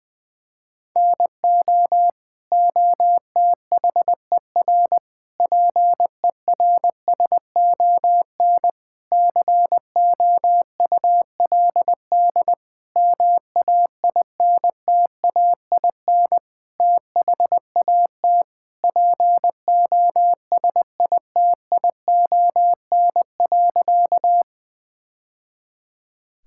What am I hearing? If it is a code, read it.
Morse code, heard as a keyed tone, about 20 wpm, 700 Hz. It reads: NO OTHER PERSON COULD MAINTAIN THAT POSITION.
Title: Sentences from Top 1000 Words - Review Part 5 - 20wpm